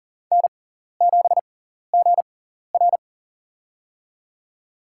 Morse code, heard as a keyed tone, about 37 wpm, spaced as for 14 wpm, 700 Hz. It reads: N7GR
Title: USA Call Signs - 4 Characters - 6x Character Spacing - 40wpm